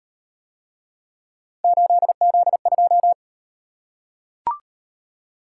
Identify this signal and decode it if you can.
Morse code, heard as a keyed tone, about 34 wpm, 700 Hz. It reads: 8Z2